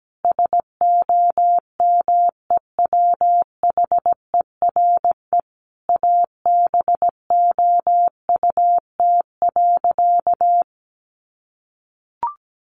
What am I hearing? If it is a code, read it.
Morse code, heard as a keyed tone, about 17 wpm, 700 Hz. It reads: SOMEWHERE ABOUT.
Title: Sentences from Top 1400 Words 17wpm